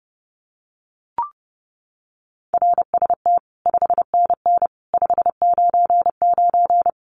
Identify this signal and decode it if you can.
Morse code, heard as a keyed tone, about 29 wpm, 700 Hz. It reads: RST 5NN 599
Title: Common QSO Phrases 30wpm